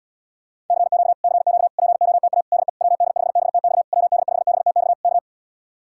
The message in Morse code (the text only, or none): CQ CQ POTA DE WD5BTX WD5BTX K